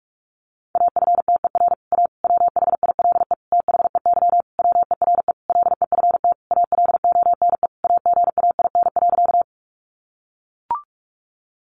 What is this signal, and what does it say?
Morse code, heard as a keyed tone, about 44 wpm, 700 Hz. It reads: AFTER A WHILE THEY WERE LEFT ALONE AGAIN.
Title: Sentences from Top 1000 Words 45wpm